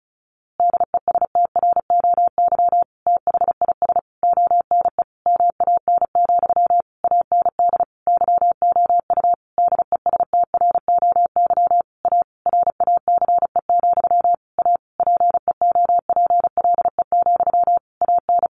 Morse code, heard as a keyed tone, about 34 wpm, 700 Hz. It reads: DESTROY THIS ONE MAN, AND YOU DESTROY A RACE, A PEOPLE, AN
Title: A Sound Of Thunder 35wpm